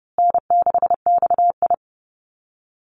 Morse code, heard as a keyed tone, about 30 wpm, 700 Hz. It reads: N6XI
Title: Call Signs Repeated 2x - 30wpm